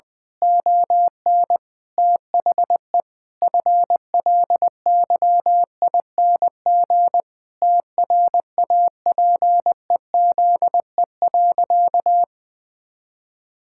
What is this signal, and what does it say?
Morse code, heard as a keyed tone, about 21 wpm, 700 Hz. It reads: ON THE FLYING TRAPEZE.
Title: Pangram Challenge v2 - 20wpm